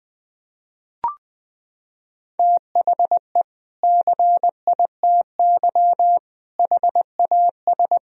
Morse code, heard as a keyed tone, about 20 wpm, 700 Hz. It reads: THE CITY HAS
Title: Sentences from Top 1000 Words - Review Part 4 - 20wpm